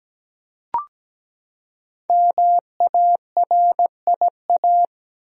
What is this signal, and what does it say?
Morse code, heard as a keyed tone, about 17 wpm, 700 Hz. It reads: MARIA
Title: USA Common Names - 17wpm